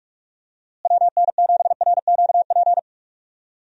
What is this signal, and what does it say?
Morse code, heard as a keyed tone, about 46 wpm, 700 Hz. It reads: WN7RQP
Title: USA Call Signs - 6 Characters - Repeated 2x - 45wpm